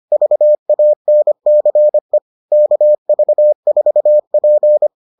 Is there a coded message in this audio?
Yes: Morse code from 0.1 to 4.9 s, heard as a keyed tone, about 25 wpm, 600 Hz. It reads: VANCE KV4P